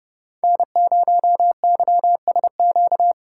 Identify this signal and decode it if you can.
Morse code, heard as a keyed tone, about 30 wpm, 700 Hz. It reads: N0YSQ
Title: K1USN SST QSO Practice - 30wpm